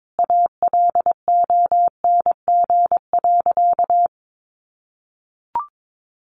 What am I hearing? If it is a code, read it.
Morse code, heard as a keyed tone, about 22 wpm, 700 Hz. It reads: ALONG.
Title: Sentences from Top 1500 Words 22wpm